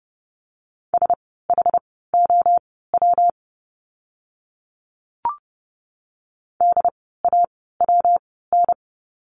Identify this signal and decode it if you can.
Morse code, heard as a keyed tone, about 30 wpm, spaced as for 17 wpm, 700 Hz. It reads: SHOW DAWN
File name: course / 30:10wpm / 115 Lesson 19f - Letter w - Words 30wpm